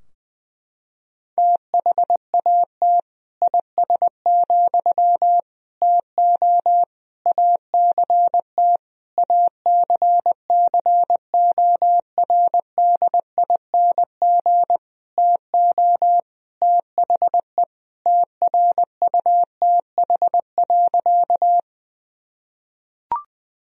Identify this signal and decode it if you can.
Morse code, heard as a keyed tone, about 19 wpm, 700 Hz. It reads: THAT IS, TO ACT ACCORDING TO THE TRUTH.
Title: Sentences from Top 1000 Words - Review Part 4 - 20wpm